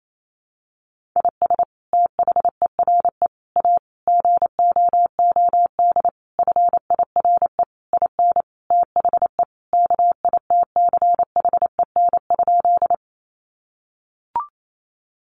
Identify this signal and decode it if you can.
Morse code, heard as a keyed tone, about 28 wpm, 700 Hz. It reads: IS THERE A GOOD FIRE IN THE KITCHEN?
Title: Sentences from Top 1400 Words 28wpm